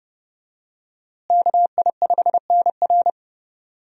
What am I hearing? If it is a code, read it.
Morse code, heard as a keyed tone, about 29 wpm, 700 Hz. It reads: KI5NR
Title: USA Call Signs - 5 Characters - Repeated 2x - 30wpm